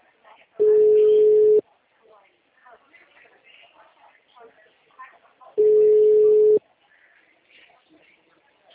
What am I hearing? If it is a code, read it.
Ringback tone (Europe, 425 Hz), with speech.